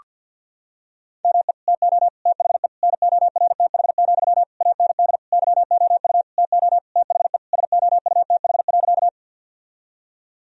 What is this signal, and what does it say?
Morse code, heard as a keyed tone, about 51 wpm, 700 Hz. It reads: ME TO THE NORTH, AND YOU TO THE SOUTH.
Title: Sentences from Top 1000 Words - Review Part 2 - 50wpm